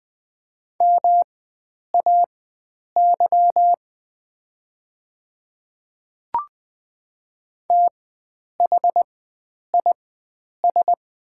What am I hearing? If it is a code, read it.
Morse code, heard as a keyed tone, about 20 wpm, spaced as for 9 wpm, 700 Hz. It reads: MAY THIS
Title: 4x Character Spacing - Top 100 Words - 20wpm